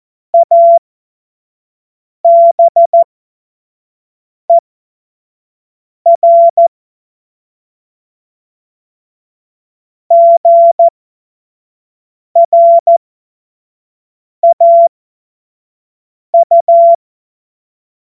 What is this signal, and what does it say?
Morse code, heard as a keyed tone, about 14 wpm, spaced as for 5 wpm, 680 Hz. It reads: ABER GRAU
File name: Morsekurs / Witze / Witz 29. November 2022 25 F70